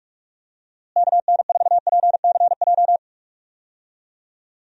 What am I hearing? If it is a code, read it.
Morse code, heard as a keyed tone, about 44 wpm, 700 Hz. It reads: KN4PCJ
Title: USA Call Signs - 6 Characters - Repeated 3x - 45wpm